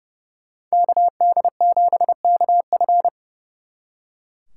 Morse code, heard as a keyed tone, about 37 wpm, 700 Hz. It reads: KD7KF
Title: Speed Racing - USA Call Signs 30wpm